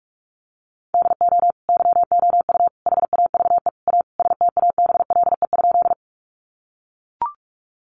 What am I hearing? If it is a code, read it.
Morse code, heard as a keyed tone, about 44 wpm, 700 Hz. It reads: DO YOU HAVE A STABLE?
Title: Sentences from Top 1500 Words 45wpm